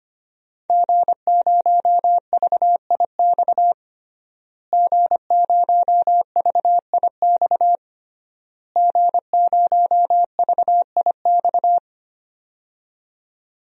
Morse code, heard as a keyed tone, about 25 wpm, 700 Hz. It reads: G0VIX G0VIX G0VIX
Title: DXCC-100 Call Signs - 5 Characters - Repeated 3x - 25wpm